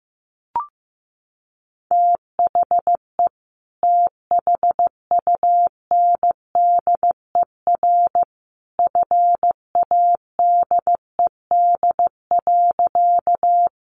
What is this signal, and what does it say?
Morse code, heard as a keyed tone, about 15 wpm, 700 Hz. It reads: THE THUNDER FADED.
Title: A Sound Of Thunder 15wpm